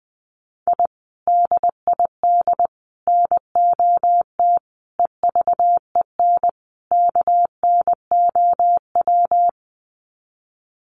Morse code, heard as a keyed tone, about 20 wpm, 700 Hz. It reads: I DID NOT EVEN KNOW
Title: Sets of 5 Words Encore 20wpm